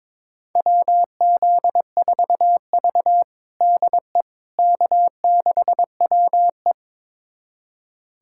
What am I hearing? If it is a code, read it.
Morse code, heard as a keyed tone, about 22 wpm, 700 Hz. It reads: WZ4V DE K6WE
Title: USA Callsign DE Callsign - 4 Characters - 22wpm